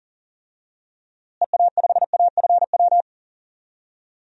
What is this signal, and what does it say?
Morse code, heard as a keyed tone, about 40 wpm, 700 Hz. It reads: EA5AFW
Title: DXCC-100 Call Signs - 6 Characters - Repeated 2x - 40wpm